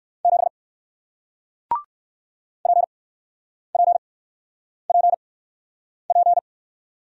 Morse code, heard as a keyed tone, about 56 wpm, spaced as for 9 wpm, 700 Hz. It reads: 7 PPPP
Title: Speed-Racing Single Letter-Number 45wpm